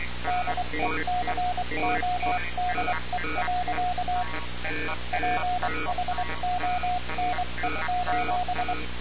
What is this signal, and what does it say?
Morse code, heard as a keyed tone, about 23 wpm, 720 Hz, with noise. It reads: DECODE. PHONE:
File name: radiocap / REC z